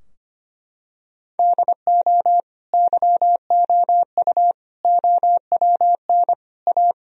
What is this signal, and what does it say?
Morse code, heard as a keyed tone, about 25 wpm, 700 Hz. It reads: DO YOU OWN A
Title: Sentences from Top 1000 Words - Review Part 2 - 25wpm